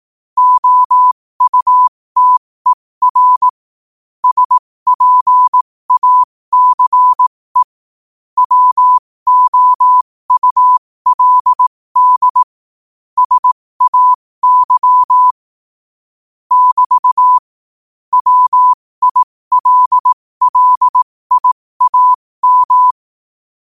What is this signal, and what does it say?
Morse code, heard as a keyed tone, about 18 wpm, spaced as for 15 wpm, 1000 Hz. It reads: OUTER SPACE WOULD SAY <BT> WILLIAM
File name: qotdfiles / 15wpm